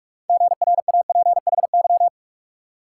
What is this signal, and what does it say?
Morse code, heard as a keyed tone, about 57 wpm, 690 Hz. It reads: GRAPHY